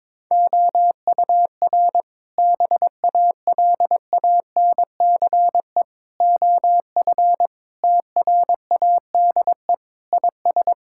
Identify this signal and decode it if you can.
Morse code, heard as a keyed tone, about 23 wpm, 700 Hz. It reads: OUR BALANCE OF TRADE IS
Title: Sentences from Top 1000 Words - Review Part 6 - 22wpm